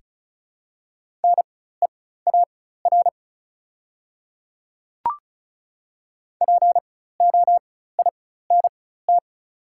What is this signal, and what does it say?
Morse code, heard as a keyed tone, about 36 wpm, spaced as for 16 wpm, 690 Hz. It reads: NEAR POINT
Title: 4x Character Spacing - Top 300 Words - 35wpm